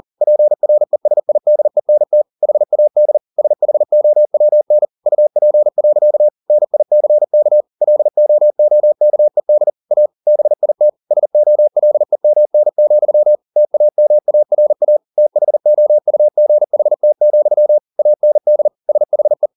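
Morse code, heard as a keyed tone, about 40 wpm, 600 Hz. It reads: PRESIDENT HAD SHOWN UP. NICK LOOKED A BIT SOLEMN, TAMARA THOUGHT, AND SHE
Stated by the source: Morse code at 40 wpm, practice text, CW med Gnister.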